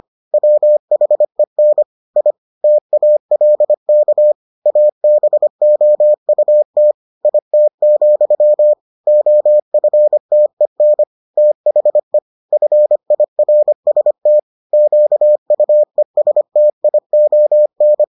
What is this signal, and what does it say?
Morse code, heard as a keyed tone, about 25 wpm, 600 Hz. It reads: WHEN I TALK ABOUT IT, OFTEN THE FIRST QUESTION